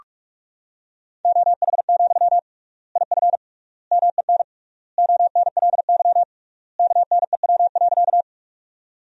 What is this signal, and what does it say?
Morse code, heard as a keyed tone, about 45 wpm, 700 Hz. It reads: OH, IF MEN ONLY KNEW.
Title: 3x Word-Spacing - Sentences from Top 500 Words - 45wpm